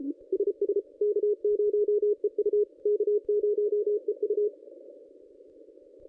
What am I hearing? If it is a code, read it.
Morse code, heard as a keyed tone, about 35 wpm, 410 Hz. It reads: SS K0EU K0EU
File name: contests / ss-cw-k5tr-op-n5rz -1100